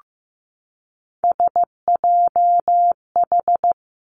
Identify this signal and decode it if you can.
Morse code, heard as a keyed tone, about 15 wpm, 700 Hz. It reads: SJH
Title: Three Challenging Letter-Number Combinations 15wpm